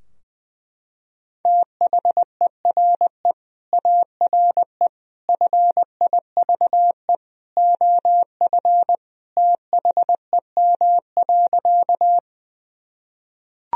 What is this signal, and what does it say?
Morse code, heard as a keyed tone, about 21 wpm, 700 Hz. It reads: THERE ARE FIVE OF THEM.
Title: Sentences from Top 1000 Words - Review Part 2 - 20wpm